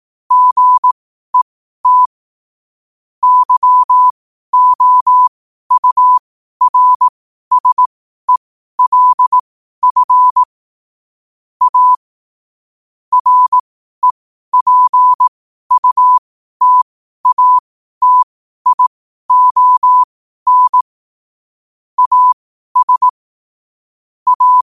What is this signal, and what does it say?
Morse code, heard as a keyed tone, about 18 wpm, spaced as for 12 wpm, 1000 Hz. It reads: GET YOURSELF A REPUTATION AS A
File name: qotdfiles / 12wpm